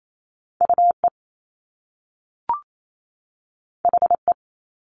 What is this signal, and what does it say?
Morse code, heard as a keyed tone, about 28 wpm, 700 Hz. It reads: U E H E
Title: Top Two Letter Combinations in Words 28wpm